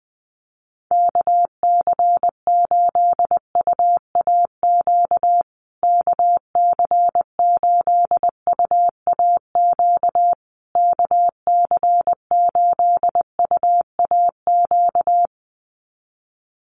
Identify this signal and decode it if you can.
Morse code, heard as a keyed tone, about 20 wpm, 700 Hz. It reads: KC8UAQ KC8UAQ KC8UAQ
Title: USA Call Signs Repeated 3x 20wpm